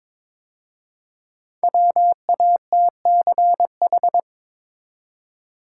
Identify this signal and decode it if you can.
Morse code, heard as a keyed tone, about 23 wpm, 700 Hz. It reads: WATCH
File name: 5 Letter Words - First 3 Letters Given - 22wpm